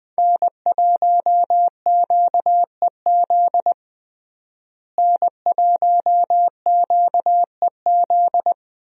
Morse code, heard as a keyed tone, about 20 wpm, 700 Hz. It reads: N1QEZ N1QEZ
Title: USA Call Signs - 5 Characters - Repeated 3x - 20wpm